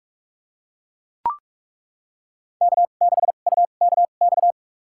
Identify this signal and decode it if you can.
Morse code, heard as a keyed tone, about 42 wpm, spaced as for 31 wpm, 700 Hz. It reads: K6UKX
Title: USA Call Signs - 2x Character Spacing - 45wpm